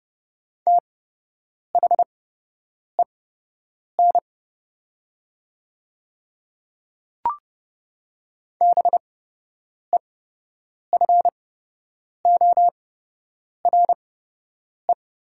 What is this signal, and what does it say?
Morse code, heard as a keyed tone, about 29 wpm, spaced as for 8 wpm, 700 Hz. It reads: THEN BEFORE
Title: 8x Character Spacing - Top 300 Words - 30wpm